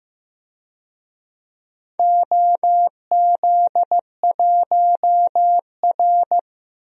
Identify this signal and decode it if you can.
Morse code, heard as a keyed tone, about 15 wpm, 700 Hz. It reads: OZ1R
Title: DXCC-100 Call Signs - 4 Characters - 15wpm